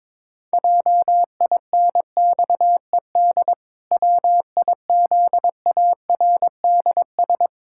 Morse code, heard as a keyed tone, about 22 wpm, 700 Hz. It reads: JINXED WIZARDS
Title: Pangram Challenge v2 - 22wpm